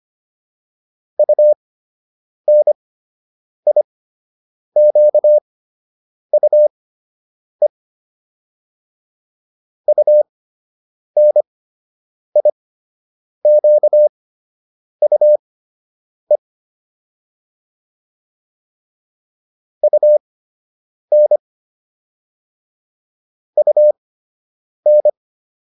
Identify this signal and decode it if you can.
Morse code, heard as a keyed tone, about 25 wpm, spaced as for 8 wpm, 600 Hz. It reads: UNIQUE UNIQUE UN UN